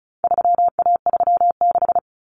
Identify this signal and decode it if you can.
Morse code, heard as a keyed tone, about 34 wpm, 700 Hz. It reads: 3A36
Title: Common Cut Numbers - 4-Digit 35wpm